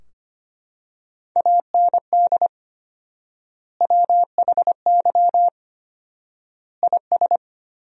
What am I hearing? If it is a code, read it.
Morse code, heard as a keyed tone, about 25 wpm, 700 Hz. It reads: AND WHY IS
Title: 4x Word-Spacing - Sentences from Top 200 Words - 25wpm